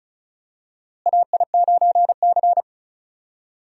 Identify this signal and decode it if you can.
Morse code, heard as a keyed tone, about 34 wpm, 700 Hz. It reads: AI9C
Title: USA Call Signs - 4 Characters - Repeated 3x - 35wpm